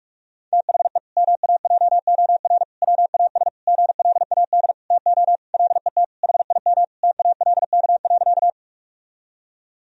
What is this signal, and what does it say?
Morse code, heard as a keyed tone, about 42 wpm, 700 Hz. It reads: THE MAJOR WAS GLAD TO LET HIM TALK.
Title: Sentences from Top 1000 Words - Review Part 4 - 45wpm